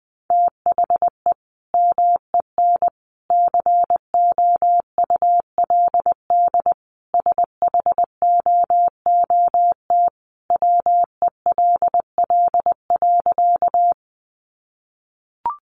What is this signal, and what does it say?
Morse code, heard as a keyed tone, about 20 wpm, 700 Hz. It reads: THE MEN COULD SHOOT WELL.
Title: Sentences from Top 1200 Words 20wpm